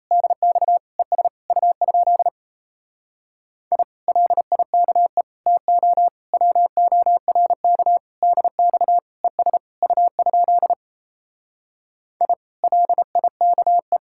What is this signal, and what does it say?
Morse code, heard as a keyed tone, about 37 wpm, 700 Hz. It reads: DX ES U? I LIKE TO WORK DX ES U? I LIKE